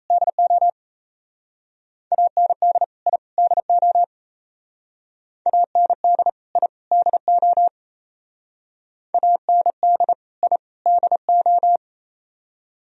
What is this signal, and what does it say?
Morse code, heard as a keyed tone, about 36 wpm, 700 Hz. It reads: DO AND I DO AND I DO AND I DO